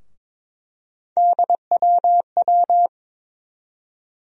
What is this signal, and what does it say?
Morse code, heard as a keyed tone, about 22 wpm, 700 Hz. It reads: DWW
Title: USA Call Sign Postfixes - 3 Letters 22wpm